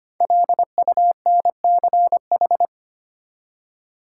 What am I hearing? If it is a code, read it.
Morse code, heard as a keyed tone, about 25 wpm, 700 Hz. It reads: LUNCH
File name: 5 Letter Words - First 3 Letters Given - 25wpm